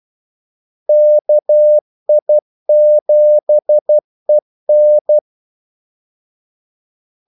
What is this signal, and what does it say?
Morse code, heard as a keyed tone, about 12 wpm, 600 Hz. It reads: KI7EN
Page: 2 ekstra ordmellemrum mellem kaldesignalerne